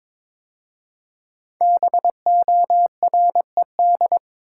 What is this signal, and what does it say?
Morse code, heard as a keyed tone, about 22 wpm, 700 Hz. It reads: BORED
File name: 5 Letter Words - First 3 Letters Given - 22wpm